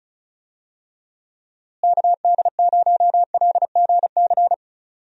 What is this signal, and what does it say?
Morse code, heard as a keyed tone, about 36 wpm, 700 Hz. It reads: KD0LGC
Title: USA Call Signs - 6 Characters - Repeated 2x - 35wpm